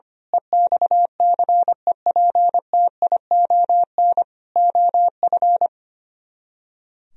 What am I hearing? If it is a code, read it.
Morse code, heard as a keyed tone, about 24 wpm, 700 Hz. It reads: EXCEPTION OF
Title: Speed-Racing - Sets of 4 Words 25wpm